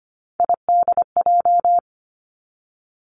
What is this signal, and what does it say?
Morse code, heard as a keyed tone, about 25 wpm, 700 Hz. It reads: IDJ
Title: Three Challenging Letter-Number Combinations 25wpm